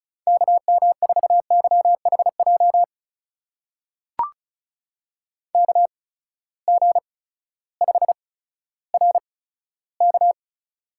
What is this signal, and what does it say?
Morse code, heard as a keyed tone, about 34 wpm, spaced as for 16 wpm, 700 Hz. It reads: KM4YHJ KG5RK
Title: USA Call Signs - 8x Character Spacing - 35wpm